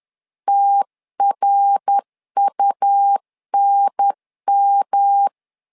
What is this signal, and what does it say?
Morse code, heard as a keyed tone, about 10 wpm, 800 Hz. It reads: TRUNM
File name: LEZIONI / 3 MWRO / 3 MWR 10wpm 1952